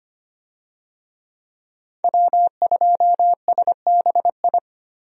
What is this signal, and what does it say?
Morse code, heard as a keyed tone, about 25 wpm, 700 Hz. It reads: W2SBI
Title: USA Call Signs - 5 Characters - Repeated 3x - 25wpm